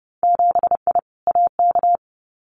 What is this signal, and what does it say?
Morse code, heard as a keyed tone, about 30 wpm, 700 Hz. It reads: 7I AK
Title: Winter Field Day Exchanges Encore 30wpm